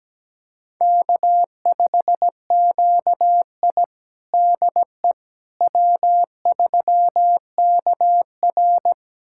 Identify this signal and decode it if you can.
Morse code, heard as a keyed tone, about 17 wpm, 700 Hz. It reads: K5QI DE W3KR
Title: USA Callsign DE Callsign - 4 Characters - 17wpm